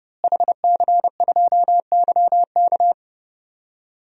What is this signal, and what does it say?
Morse code, heard as a keyed tone, about 31 wpm, 700 Hz. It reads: HC2YK
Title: DXCC-100 Call Signs - 5 Characters - Repeated 2x - 30wpm